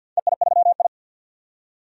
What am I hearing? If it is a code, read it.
Morse code, heard as a keyed tone, about 49 wpm, 700 Hz. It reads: EI3I
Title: DXCC-100 Call Signs - 4 Characters - 50wpm